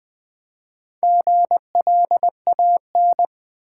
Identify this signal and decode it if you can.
Morse code, heard as a keyed tone, about 20 wpm, 700 Hz. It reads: GLAN
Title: Build A Word - 2 Letters At A Time - 6-Letter Words - 20wpm